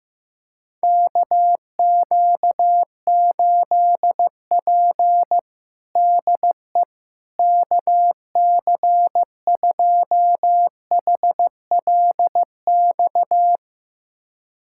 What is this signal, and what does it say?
Morse code, heard as a keyed tone, about 15 wpm, 700 Hz. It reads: KQ8P DE KC2HLX
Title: USA Callsign DE Callsign 15wpm